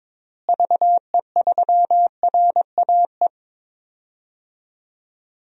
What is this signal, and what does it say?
Morse code, heard as a keyed tone, about 22 wpm, 700 Hz. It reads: VE3RAE
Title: DXCC-100 Call Signs - 6 Characters - Repeated 2x - 22wpm